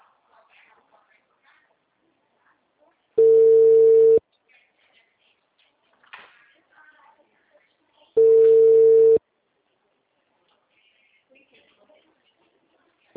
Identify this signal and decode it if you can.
Ringback tone (China, 450 Hz), with speech.
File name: out- -2021